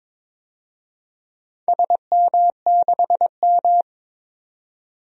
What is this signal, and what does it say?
Morse code, heard as a keyed tone, about 22 wpm, 700 Hz. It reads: SM6M
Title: DXCC-100 Call Signs - 4 Characters - 22wpm